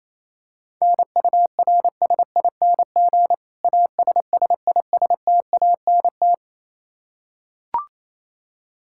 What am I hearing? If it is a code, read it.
Morse code, heard as a keyed tone, about 27 wpm, 700 Hz. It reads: NURSING ASSISTANT
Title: Practice QSO - Exchange 3 - More Info - 28wpm